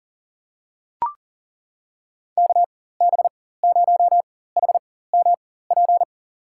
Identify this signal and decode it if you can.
Morse code, heard as a keyed tone, about 38 wpm, spaced as for 18 wpm, 700 Hz. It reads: KB0HMP
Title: USA Call Signs - 6 Characters - 4x Character Spacing - 40wpm